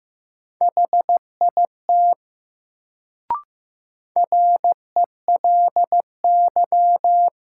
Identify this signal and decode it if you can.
Morse code, heard as a keyed tone, about 15 wpm, 700 Hz. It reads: HIT RELY
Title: Sentences from Top 1200 Words - New Word Drill - 15wpm